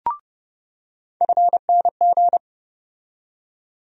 Morse code, heard as a keyed tone, about 30 wpm, 700 Hz. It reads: FNG
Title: USA Call Sign Postfixes - 3 Letters 30wpm